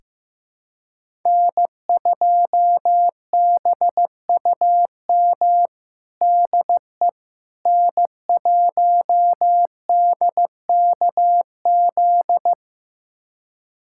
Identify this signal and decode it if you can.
Morse code, heard as a keyed tone, about 15 wpm, 700 Hz. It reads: N2BUM DE N1DKZ
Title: USA Callsign DE Callsign - 5 Characters - 15wpm